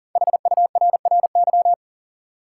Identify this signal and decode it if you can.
Morse code, heard as a keyed tone, about 40 wpm, 700 Hz. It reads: HURRY
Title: Five Letter Words - 40wpm